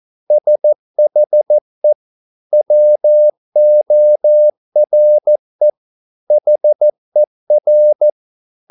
Morse code, heard as a keyed tone, about 14 wpm, 600 Hz. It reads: SHE WORE HER